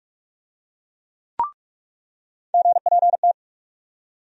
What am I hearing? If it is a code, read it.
Morse code, heard as a keyed tone, about 46 wpm, 700 Hz. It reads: GPT
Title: USA Call Sign Postfixes - 3 Letters 45wpm